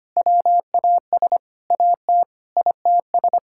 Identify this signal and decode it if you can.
Morse code, heard as a keyed tone, about 25 wpm, 700 Hz. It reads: WAS AT ITS
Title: Sentences from Top 1600 Words 25wpm